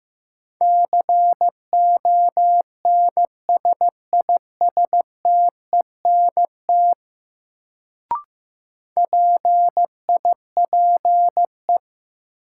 Morse code, heard as a keyed tone, about 15 wpm, 700 Hz. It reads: CONSISTENT PIPE
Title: Sentences from Top 1500 Words - New Word Drill Alt - 15wpm